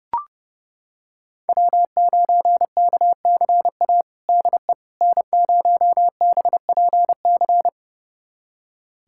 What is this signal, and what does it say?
Morse code, heard as a keyed tone, about 32 wpm, 700 Hz. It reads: W9KCA DE N0BPC
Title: USA Callsign DE Callsign - 5 Characters - 30wpm